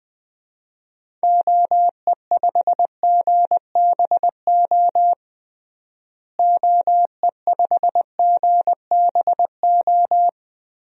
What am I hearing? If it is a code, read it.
Morse code, heard as a keyed tone, about 20 wpm, 700 Hz. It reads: OE5GBO OE5GBO
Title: DXCC-100 Call Signs - 6 Characters - Repeated 2x - 20wpm